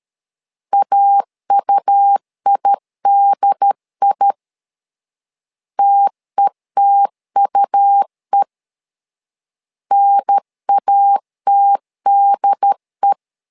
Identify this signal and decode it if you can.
Morse code, heard as a keyed tone, about 13 wpm, 800 Hz. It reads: AUIDI TETUE NATDE
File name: LEZIONI / 2 UHDN / 2 UHDN 12wpm 1456